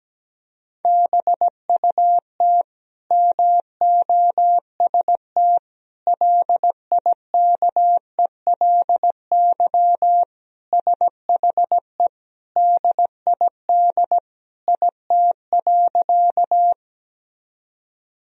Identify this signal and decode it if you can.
Morse code, heard as a keyed tone, about 17 wpm, 700 Hz. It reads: BUT MOST LIKELY SHE DID IT.
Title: Sentences from Top 1000 Words - Review Part 4 - 17wpm